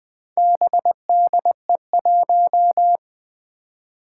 Morse code, heard as a keyed tone, about 20 wpm, 700 Hz. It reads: BDE1
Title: Four Challenging Letter-Number Combinations 20wpm